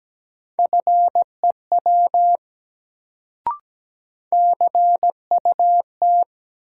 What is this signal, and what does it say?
Morse code, heard as a keyed tone, about 17 wpm, 700 Hz. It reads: FEW CUT
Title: Sentences from Top 300 Words - New Word Drill - 17wpm